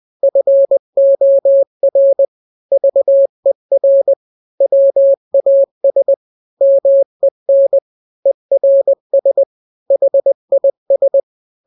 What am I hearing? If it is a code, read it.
Morse code, heard as a keyed tone, about 20 wpm, 550 Hz. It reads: FOR VER WAS MEN ERS HIS
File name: n-grams / 3-grams 20 18 WPM